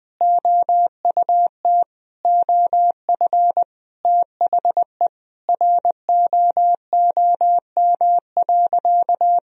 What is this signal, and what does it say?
Morse code, heard as a keyed tone, about 20 wpm, 700 Hz. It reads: OUT OF THE ROOM.
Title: Sentences from Top 500 Words and Abbreviations 20wpm